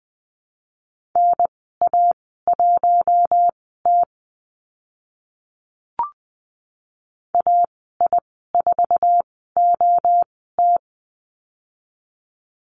Morse code, heard as a keyed tone, about 20 wpm, spaced as for 15 wpm, 700 Hz. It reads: NA1T AI4OT
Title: 069 Lesson 12g - Letter d - Call Signs 20wpm